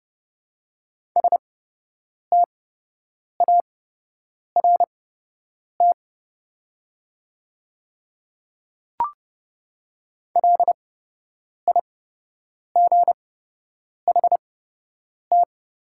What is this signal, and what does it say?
Morse code, heard as a keyed tone, about 30 wpm, spaced as for 8 wpm, 700 Hz. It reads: START LIGHT
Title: 8x Character Spacing - Top 300 Words - 30wpm